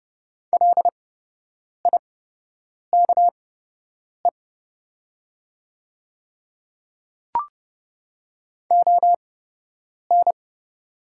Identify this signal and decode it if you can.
Morse code, heard as a keyed tone, about 30 wpm, spaced as for 8 wpm, 700 Hz. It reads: LIKE ON
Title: 8x Character Spacing - Top 100 Words - 30wpm